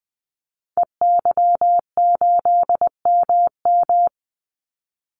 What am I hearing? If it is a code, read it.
Morse code, heard as a keyed tone, about 20 wpm, 700 Hz. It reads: EY8MM
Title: Call Signs Repeated 3x - 20wpm